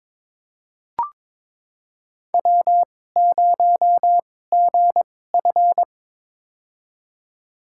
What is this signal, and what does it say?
Morse code, heard as a keyed tone, about 22 wpm, spaced as for 16 wpm, 700 Hz. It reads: W0GF
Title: USA Call Signs - 4 Characters - 2x Character Spacing - 22wpm